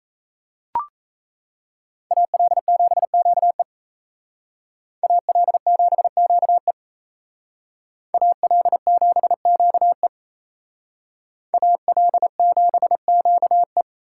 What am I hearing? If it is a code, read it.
Morse code, heard as a keyed tone, about 35 wpm, 700 Hz. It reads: AL7QE AL7QE AL7QE AL7QE